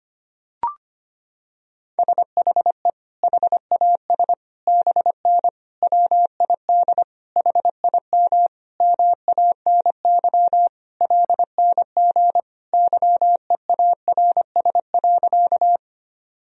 Morse code, heard as a keyed tone, about 25 wpm, 700 Hz. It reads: SHE HAS BN WID HIM MANY LNG YEARS.
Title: Sentences from Top 200 Words and Abbreviations 25wpm